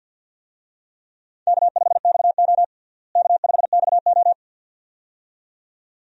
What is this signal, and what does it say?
Morse code, heard as a keyed tone, about 52 wpm, 700 Hz. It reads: K5XO K5XO
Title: USA Call Signs - 4 Characters - Repeated 2x - 50wpm